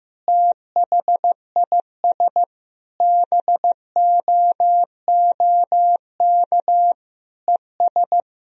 Morse code, heard as a keyed tone, about 15 wpm, 700 Hz. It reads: THIS BOOK ES
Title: Sentences from Top 400 Words and Abbreviations 15wpm